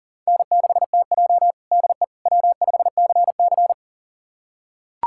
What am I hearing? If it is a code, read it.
Morse code, heard as a keyed tone, about 40 wpm, 700 Hz. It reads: N6TJ DE W5CC
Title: USA Callsign DE Callsign - 4 Characters - 40wpm